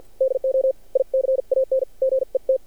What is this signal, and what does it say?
Morse code, heard as a keyed tone, about 50 wpm, spaced as for 43 wpm, 530 Hz. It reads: DO I CAN GET
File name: archived / CRSnet